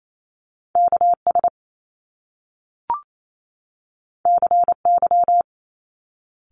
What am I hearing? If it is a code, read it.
Morse code, heard as a keyed tone, about 28 wpm, 700 Hz. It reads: KS CY
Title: Top Two Letter Combinations in Words 28wpm